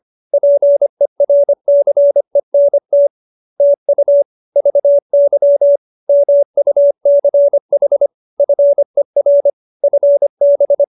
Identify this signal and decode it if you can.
Morse code, heard as a keyed tone, about 25 wpm, 570 Hz. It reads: PERCENT TU VY MUCH FER FB